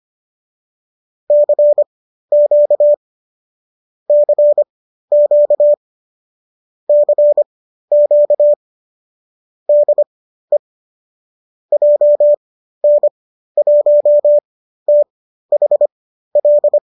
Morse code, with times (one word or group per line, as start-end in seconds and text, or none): 1.3-2.9 CQ
4.1-5.7 CQ
6.9-8.5 CQ
9.7-10.6 DE
11.7-16.8 JN1THL